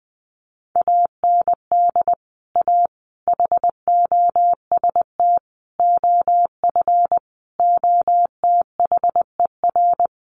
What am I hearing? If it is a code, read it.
Morse code, heard as a keyed tone, about 20 wpm, 700 Hz. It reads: AND A HOST OF OTHER
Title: Sets of 5 Words Encore2 20wpm